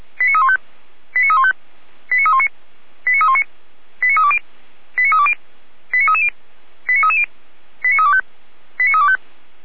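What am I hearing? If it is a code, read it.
Five-tone selective call (ZVEI), heard as quick runs of five tones, tones of about 80 ms. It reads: 89315 89315 89319 89319 89320 89320 89330 89330 89325 89325
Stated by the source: ZVEI five-tone selective call received on an SDR, fire brigade Orlamünde (feuerwehr-orlamuende), Germany.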